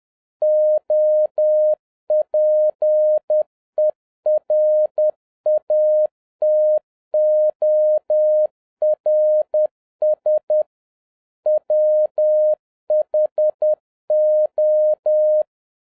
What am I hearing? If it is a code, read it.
Morse code, heard as a keyed tone, about 10 wpm, 610 Hz. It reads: OPERATORS WHO
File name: 2011-07-mp3 / n21 GREETINGS FROM SINGAPORE 10wpm